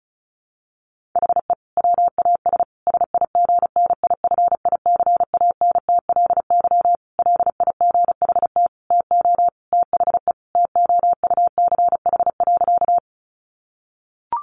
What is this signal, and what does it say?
Morse code, heard as a keyed tone, about 34 wpm, 700 Hz. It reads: HE WAS SIGNIFICANTLY LIGHT TO THE TOUCH.
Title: Sentences from Top 1500 Words 35wpm